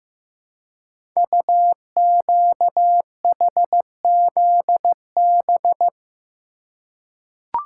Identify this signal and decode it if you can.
Morse code, heard as a keyed tone, about 15 wpm, 700 Hz. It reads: UQHZB
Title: Code Groups - 5 Letters - 15wpm